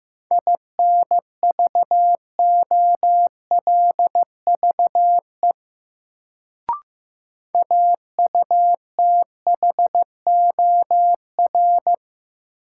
Morse code, heard as a keyed tone, about 15 wpm, 700 Hz. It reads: INVOLVE AUTHOR
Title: Sentences from Top 1200 Words - New Word Drill - 15wpm